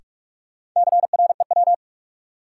Morse code, heard as a keyed tone, about 45 wpm, 700 Hz. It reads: CREW
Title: Sentences from Top 1500 Words - New Word Drill - 45wpm